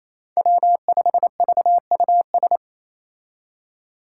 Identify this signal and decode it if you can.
Morse code, heard as a keyed tone, about 28 wpm, 700 Hz. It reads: W5VUS
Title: Five Challenging Letter-Number Combinations 28wpm